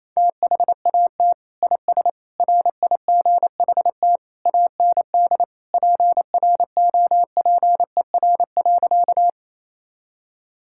Morse code, heard as a keyed tone, about 28 wpm, 700 Hz. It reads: THAT IS RIGHT AND PROPER.
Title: Sentences from Top 1000 Words - Review Part 5 - 28wpm